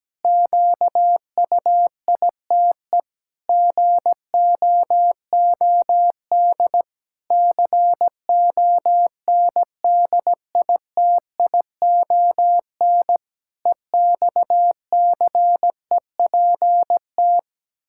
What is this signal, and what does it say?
Morse code, heard as a keyed tone, about 17 wpm, 700 Hz. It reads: QUITE GOOD CONDITION EXCEPT
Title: Pangram Challenge v2 - 17wpm